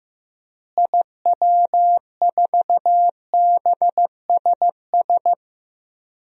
Morse code, heard as a keyed tone, about 16 wpm, 700 Hz. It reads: IW4BSS
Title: DXCC-100 Call Signs - 6 Characters - Repeated 3x - 15wpm